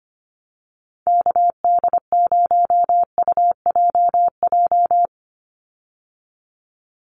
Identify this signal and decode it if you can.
Morse code, heard as a keyed tone, about 25 wpm, 700 Hz. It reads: KD0UJJ
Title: USA Call Signs 25wpm